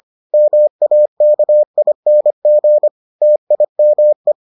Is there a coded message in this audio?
Yes: Morse code from 0.3 to 4.3 s, heard as a keyed tone, about 25 wpm, 600 Hz. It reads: MAKING TIME